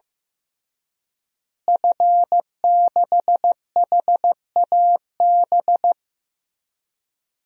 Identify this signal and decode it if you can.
Morse code, heard as a keyed tone, about 15 wpm, 700 Hz. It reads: F6HAB
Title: DXCC-100 Call Signs - 5 Characters - Repeated 3x - 15wpm